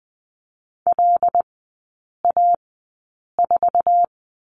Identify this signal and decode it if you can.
Morse code, heard as a keyed tone, about 20 wpm, spaced as for 8 wpm, 700 Hz. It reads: LA4
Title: 074 Lesson 13e - Letter L - Three Characters 20wpm